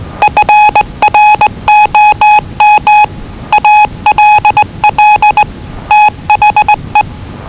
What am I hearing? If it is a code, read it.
Morse code, heard as a keyed tone, about 20 wpm, 840 Hz. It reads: FROM ALL THE
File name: WellingtonRadioZlwCloseDownOnHfAndMf / 01-Zlw 4277khz 417.5 qrt until 0925z